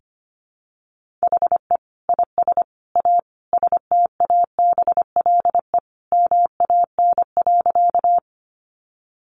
Morse code, heard as a keyed tone, about 25 wpm, 700 Hz. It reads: HE IS A STABLE MAN.
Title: Sentences from Top 1500 Words 25wpm